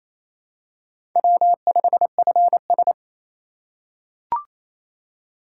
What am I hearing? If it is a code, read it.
Morse code, heard as a keyed tone, about 27 wpm, 700 Hz. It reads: W5FS
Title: USA Call Signs - 4 Characters - 4x Character Spacing - 28wpm